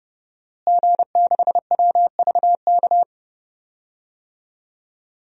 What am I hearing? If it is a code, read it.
Morse code, heard as a keyed tone, about 30 wpm, 700 Hz. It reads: G6WVK
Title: DXCC-100 Call Signs - 5 Characters - Repeated 3x - 30wpm